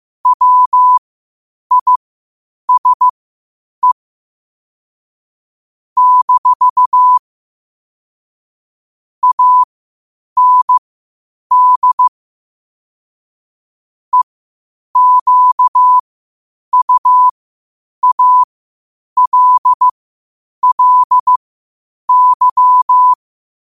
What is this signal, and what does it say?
Morse code, heard as a keyed tone, about 15 wpm, spaced as for 8 wpm, 1000 Hz. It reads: WISE - AND EQUALLY
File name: qotdfiles / 8wpm